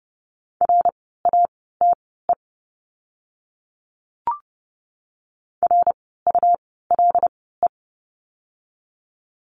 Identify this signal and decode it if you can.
Morse code, heard as a keyed tone, about 30 wpm, spaced as for 17 wpm, 700 Hz. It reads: RATE RULE
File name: Lesson 17f - Letter u - Words 30wpm